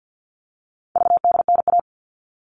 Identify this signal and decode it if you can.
Morse code, heard as a keyed tone, about 50 wpm, 700 Hz. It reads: VDNA
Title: All Cut Numbers - 4-Digit 50wpm